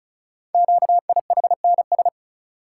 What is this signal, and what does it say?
Morse code, heard as a keyed tone, about 34 wpm, 700 Hz. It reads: QIHNS
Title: Code Groups - 5 Letters - 35wpm